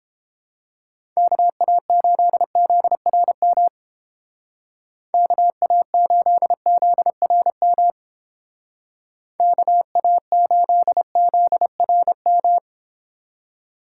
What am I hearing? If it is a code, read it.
Morse code, heard as a keyed tone, about 30 wpm, 700 Hz. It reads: KA8ZRM KA8ZRM KA8ZRM